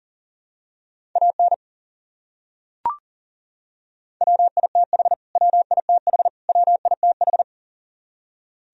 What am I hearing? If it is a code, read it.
Morse code, heard as a keyed tone, about 37 wpm, 700 Hz. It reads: AN WITH WITH WITH
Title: Top 100 Words Repeated 40wpm